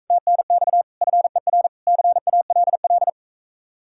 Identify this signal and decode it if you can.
Morse code, heard as a keyed tone, about 46 wpm, 700 Hz. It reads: TNX FER CALL